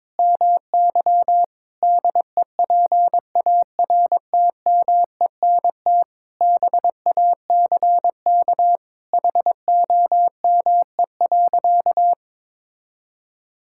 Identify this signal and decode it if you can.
Morse code, heard as a keyed tone, about 22 wpm, 700 Hz. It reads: MY DEPARTMENT BACK HOME.
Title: Sentences from Top 1000 Words - Review Part 6 - 22wpm